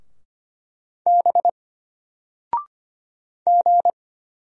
Morse code, heard as a keyed tone, about 25 wpm, 700 Hz. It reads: B G
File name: Single-Letter-Number-25wpm